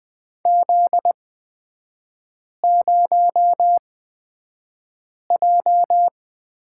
Morse code, heard as a keyed tone, about 20 wpm, spaced as for 5 wpm, 700 Hz. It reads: Z0J
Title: 243 Lesson 39e - Letter z - Three Characters 20wpm